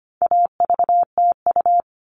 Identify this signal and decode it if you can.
Morse code, heard as a keyed tone, about 25 wpm, 700 Hz. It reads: AVTU
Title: All Cut Numbers - 4-Digit 25wpm